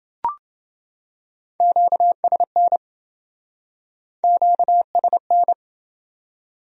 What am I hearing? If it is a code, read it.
Morse code, heard as a keyed tone, about 28 wpm, 700 Hz. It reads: QSN QSN